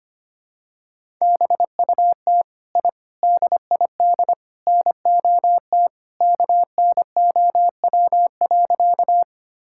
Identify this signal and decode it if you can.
Morse code, heard as a keyed tone, about 25 wpm, 700 Hz. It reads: BUT I DID NOT KNOW.
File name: practice / 25wpm / Sentences with 4-Word Phrases - 25wpm